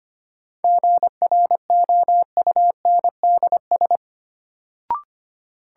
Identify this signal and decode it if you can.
Morse code, heard as a keyed tone, about 25 wpm, 700 Hz. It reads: GROUNDS
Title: Sentences from Top 1300 Words - New Word Drill Alt - 25wpm